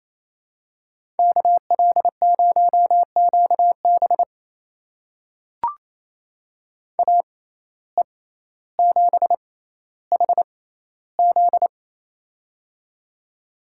Morse code, heard as a keyed tone, about 27 wpm, spaced as for 14 wpm, 700 Hz. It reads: KL0QB AE7HZ
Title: USA Call Signs - 5 Characters - 6x Character Spacing - 28wpm